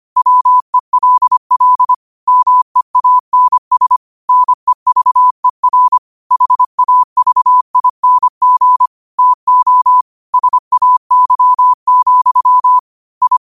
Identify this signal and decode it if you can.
Morse code, heard as a keyed tone, about 25 wpm, 1000 Hz. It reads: WELL MEANS NEVER HAVING TO SAY, I
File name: qotdfiles / 25wpm